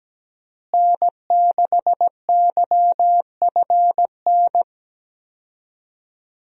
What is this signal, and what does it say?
Morse code, heard as a keyed tone, about 17 wpm, 700 Hz. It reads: N6YFN
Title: USA Call Signs - 5 Characters - Repeated 3x - 17wpm